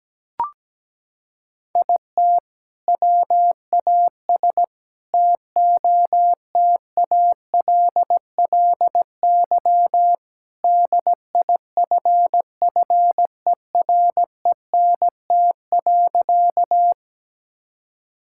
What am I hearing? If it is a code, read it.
Morse code, heard as a keyed tone, about 17 wpm, 700 Hz. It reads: IT WAS TOTALLY DIFFERENT.
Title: Sentences from Top 1600 Words 17wpm